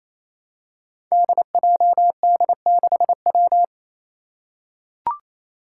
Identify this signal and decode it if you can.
Morse code, heard as a keyed tone, about 28 wpm, 700 Hz. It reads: DJD6W
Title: Five Challenging Letter-Number Combinations 28wpm